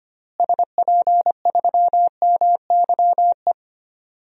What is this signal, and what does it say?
Morse code, heard as a keyed tone, about 25 wpm, 700 Hz. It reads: SP3MYE
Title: DXCC-100 Call Signs - 6 Characters - Repeated 3x - 25wpm